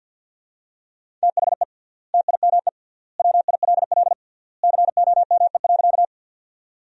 Morse code, heard as a keyed tone, about 52 wpm, 700 Hz. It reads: THE TIME WILL COME.
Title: 3x Word-Spacing - Sentences from Top 100 Words - 50wpm